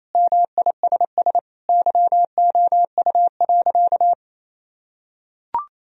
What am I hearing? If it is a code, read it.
Morse code, heard as a keyed tone, about 27 wpm, 700 Hz. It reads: MISS YOU.
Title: Sentences from Top 1000 Words - Review Part 3 - 28wpm